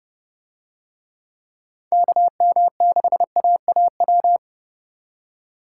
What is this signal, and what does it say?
Morse code, heard as a keyed tone, about 30 wpm, 700 Hz. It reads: KM6AAW